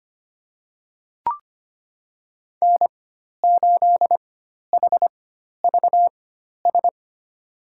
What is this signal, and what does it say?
Morse code, heard as a keyed tone, about 25 wpm, spaced as for 12 wpm, 700 Hz. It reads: N8HVS
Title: USA Call Signs - 5 Characters - 4x Character Spacing - 25wpm